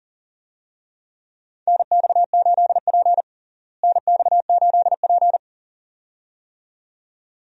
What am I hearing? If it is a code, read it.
Morse code, heard as a keyed tone, about 40 wpm, 700 Hz. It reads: NX8P NX8P
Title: USA Call Signs - 4 Characters - Repeated 2x - 40wpm